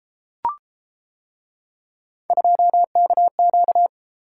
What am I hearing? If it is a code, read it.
Morse code, heard as a keyed tone, about 31 wpm, 700 Hz. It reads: 2KQ